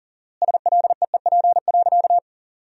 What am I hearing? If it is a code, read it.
Morse code, heard as a keyed tone, about 40 wpm, 700 Hz. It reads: SLEEP.
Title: 4x Word-Spacing - Sentences from Top 500 Words - 40wpm